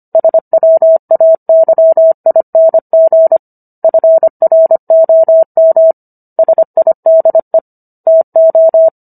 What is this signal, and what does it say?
Morse code, heard as a keyed tone, about 25 wpm, 650 Hz. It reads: SWAYING FROM SIDE TO